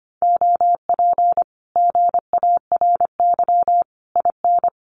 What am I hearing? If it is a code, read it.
Morse code, heard as a keyed tone, about 25 wpm, 700 Hz. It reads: OP GARY IN
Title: Common QSO Phrases Encore 25wpm